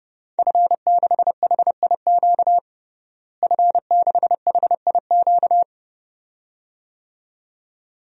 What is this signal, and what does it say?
Morse code, heard as a keyed tone, about 30 wpm, 700 Hz. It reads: F6HIQ F6HIQ
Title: DXCC-100 Call Signs - 5 Characters - Repeated 2x - 30wpm